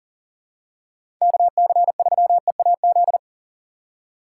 Morse code, heard as a keyed tone, about 40 wpm, 700 Hz. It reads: KC3EAZ
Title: USA Call Signs - 6 Characters - 6x Character Spacing - 40wpm